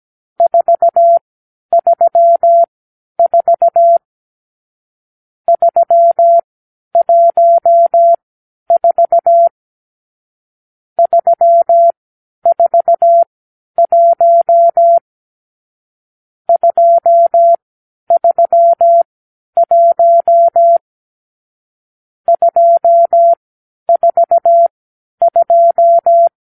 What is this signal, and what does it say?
Morse code, heard as a keyed tone, about 17 wpm, spaced as for 10 wpm, 680 Hz. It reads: 434 314 341 231 242
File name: k7qo / Lesson 46 (1-4)